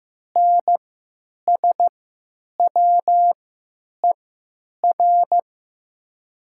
Morse code, heard as a keyed tone, about 15 wpm, spaced as for 9 wpm, 700 Hz. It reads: NSWER
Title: 3x Character Spacing - Top 300 Words - 15wpm